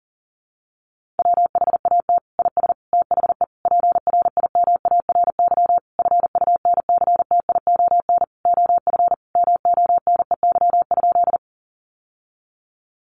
Morse code, heard as a keyed tone, about 40 wpm, 700 Hz. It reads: WHAT IS THE PRIMARY FUNCTION OF MONEY?
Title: Sentences from Top 1100 Words 40wpm